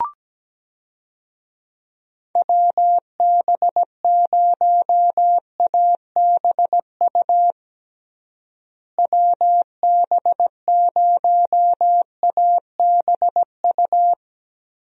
Morse code, heard as a keyed tone, about 17 wpm, 700 Hz. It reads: WB0ABU WB0ABU
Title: USA Call Signs - 6 Characters - Repeated 3x - 17wpm